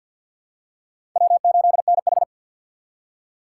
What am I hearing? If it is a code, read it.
Morse code, heard as a keyed tone, about 51 wpm, 700 Hz. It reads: W8NH
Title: USA Call Signs - 4 Characters - 3x Character Spacing - 50wpm